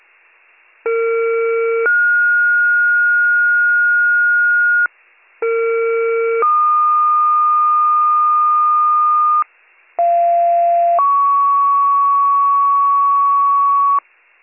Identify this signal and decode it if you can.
Two-tone page, heard as a one-second tone, then a three-second tone, about 460 then 1430 Hz, about 460 then 1220 Hz, about 690 then 1120 Hz. It is heard from 0.9 to 14.0 s.